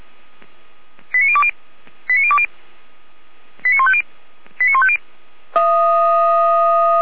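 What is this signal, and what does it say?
Five-tone selective call (ZVEI), heard as quick runs of five tones, tones of about 70 ms. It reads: 89020 89020 89150 89150
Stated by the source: ZVEI five-tone call received by SDR.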